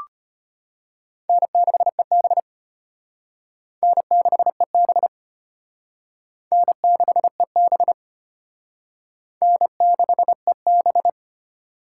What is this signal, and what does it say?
Morse code, heard as a keyed tone, about 33 wpm, 700 Hz. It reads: N6EB N6EB N6EB N6EB